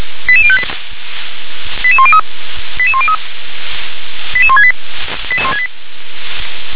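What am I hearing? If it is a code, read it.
Five-tone selective call (ZVEI), heard as quick runs of five tones, tones of about 70 ms. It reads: 88058 88103 88103 88168
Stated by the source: ZVEI five-tone call 88006 received by SDR.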